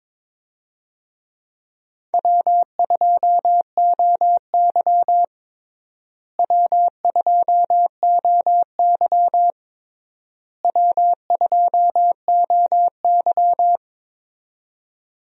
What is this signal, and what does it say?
Morse code, heard as a keyed tone, about 22 wpm, 700 Hz. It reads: W2OY W2OY W2OY
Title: USA Call Signs - 4 Characters - Repeated 3x - 22wpm